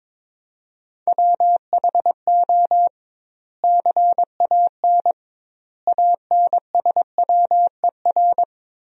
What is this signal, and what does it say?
Morse code, heard as a keyed tone, about 22 wpm, 700 Hz. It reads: WHO CAN ANSWER
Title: 2x Word-Spacing - Sentences from Top 300 Words - 22wpm